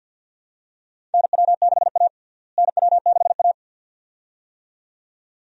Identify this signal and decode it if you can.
Morse code, heard as a keyed tone, about 51 wpm, 700 Hz. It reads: NW6A NW6A
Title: USA Call Signs - 4 Characters - Repeated 2x - 50wpm